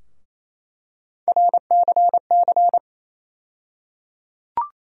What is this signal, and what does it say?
Morse code, heard as a keyed tone, about 28 wpm, 700 Hz. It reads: RCC
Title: USA Call Sign Postfixes - 3 Letters 28wpm